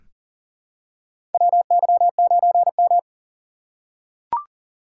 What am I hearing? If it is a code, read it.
Morse code, heard as a keyed tone, about 40 wpm, 700 Hz. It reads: WY9M
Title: USA Call Signs - 2x Character Spacing - 40wpm